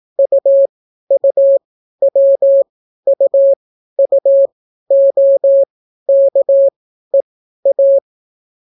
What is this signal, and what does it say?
Morse code, heard as a keyed tone, about 18 wpm, spaced as for 12 wpm, 550 Hz. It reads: UUWUUOKEA